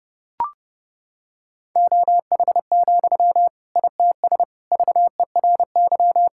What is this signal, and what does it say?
Morse code, heard as a keyed tone, about 28 wpm, 700 Hz. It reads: OH, ITS VERY
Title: Sentences from Top 1000 Words - Review Part 6 - 30wpm